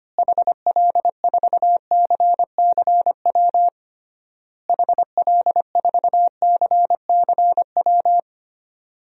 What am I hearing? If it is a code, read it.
Morse code, heard as a keyed tone, about 25 wpm, 700 Hz. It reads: HL4CCW HL4CCW
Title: DXCC-100 Call Signs - 6 Characters - Repeated 3x - 25wpm